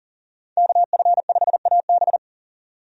Morse code, heard as a keyed tone, about 40 wpm, 700 Hz. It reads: KF5AB
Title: USA Call Signs - 5 Characters - 40wpm